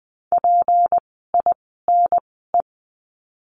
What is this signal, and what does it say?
Morse code, heard as a keyed tone, about 20 wpm, spaced as for 15 wpm, 700 Hz. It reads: PINE
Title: 153 Lesson 25f - Letter p - Words 20wpm